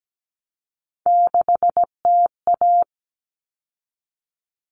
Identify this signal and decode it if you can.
Morse code, heard as a keyed tone, about 17 wpm, 700 Hz. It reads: 6TA
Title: All Cut Numbers - 3-Digit 17wpm